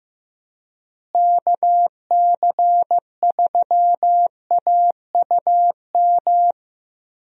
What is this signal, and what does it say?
Morse code, heard as a keyed tone, about 15 wpm, 700 Hz. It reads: KC3AUM
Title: USA Call Signs - 6 Characters - Repeated 2x - 15wpm